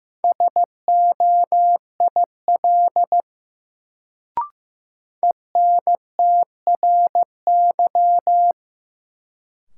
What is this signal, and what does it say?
Morse code, heard as a keyed tone, about 15 wpm, 700 Hz. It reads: SOIL ENTRY
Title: Sentences from Top 1500 Words - New Word Drill Alt - 15wpm